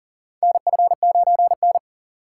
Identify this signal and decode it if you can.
Morse code, heard as a keyed tone, about 40 wpm, 700 Hz. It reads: NF9N
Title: USA Call Signs - 4 Characters - 3x Character Spacing - 40wpm